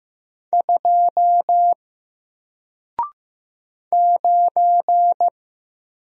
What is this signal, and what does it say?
Morse code, heard as a keyed tone, about 15 wpm, 700 Hz. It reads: 2 9
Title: Single Letter-Number Alt 15wpm